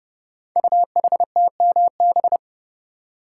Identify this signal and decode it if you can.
Morse code, heard as a keyed tone, about 30 wpm, 690 Hz. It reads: UHTMB
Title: Code Groups - 5 Letter:Numbers - 30wpm